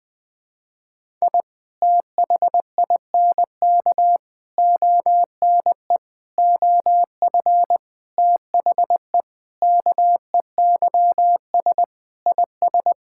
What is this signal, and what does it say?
Morse code, heard as a keyed tone, about 20 wpm, 700 Hz. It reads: I THINK ONE OF THE KEYS IS
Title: Sentences with 5-Word Phrases - 20wpm